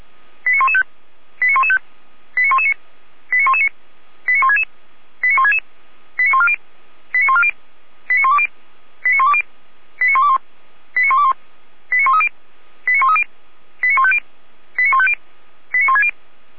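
Five-tone selective call (ZVEI), heard as quick runs of five tones, tones of about 70 ms. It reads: 89115 89115 89119 89119 89166 89166 89140 89140 89120 89120 89121 89121 89130 89130 89150 89150 89160
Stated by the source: ZVEI five-tone call received by SDR.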